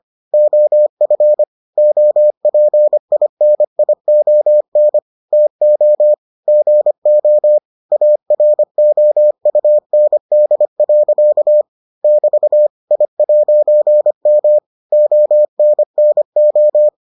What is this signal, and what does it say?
Morse code, heard as a keyed tone, about 25 wpm, 600 Hz. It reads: OF OPINION TO GO AROUND. <BT> I'M ONNO